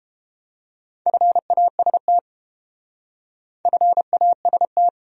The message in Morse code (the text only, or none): FAST FAST